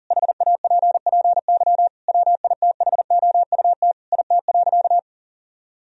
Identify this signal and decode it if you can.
Morse code, heard as a keyed tone, about 40 wpm, 700 Hz. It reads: HAPPY WITHOUT IT.
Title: Sentences from Top 1000 Words - Review Part 3 - 40wpm